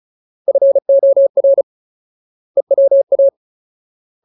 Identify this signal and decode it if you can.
Morse code, heard as a keyed tone, about 35 wpm, 550 Hz. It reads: FOR EWA
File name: Ultimate Word List / 35WPM Ultimate